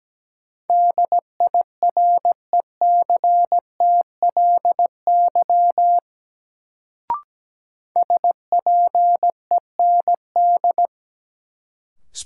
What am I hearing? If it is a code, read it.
Morse code, heard as a keyed tone, about 17 wpm, 700 Hz. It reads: DIRECTLY SPEND
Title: Sentences from Top 800 Words - New Word Drill - 17wpm